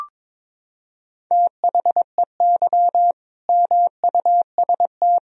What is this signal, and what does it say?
Morse code, heard as a keyed tone, about 22 wpm, 700 Hz. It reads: THEY MUST
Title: Sentences from Top 1000 Words - Review Part 3 - 22wpm